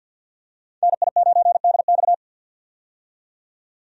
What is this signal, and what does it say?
Morse code, heard as a keyed tone, about 45 wpm, 700 Hz. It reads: NI9DX
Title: USA Call Signs - 5 Characters - 50wpm